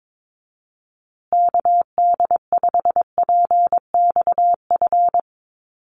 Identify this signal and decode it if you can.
Morse code, heard as a keyed tone, about 22 wpm, 700 Hz. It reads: KD5PXF
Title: USA Call Signs Repeated 3x 22wpm